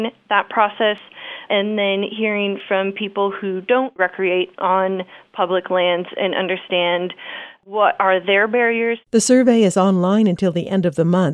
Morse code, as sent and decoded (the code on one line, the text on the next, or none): none